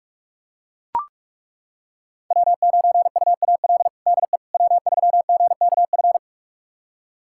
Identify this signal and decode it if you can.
Morse code, heard as a keyed tone, about 42 wpm, 700 Hz. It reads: W9UAL DE W3GKF
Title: USA Callsign DE Callsign - 5 Characters - 45wpm